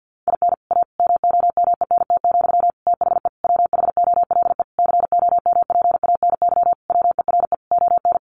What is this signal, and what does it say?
Morse code, heard as a keyed tone, about 50 wpm, 700 Hz. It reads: IN A MOMENT, THE WHOLE COMPANY WERE ON
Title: Sentences from Top 1000 Words 50wpm